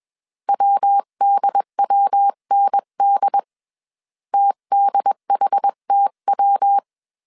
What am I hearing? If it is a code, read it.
Morse code, heard as a keyed tone, about 21 wpm, 800 Hz. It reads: WDWND TDHTW
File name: 20wpm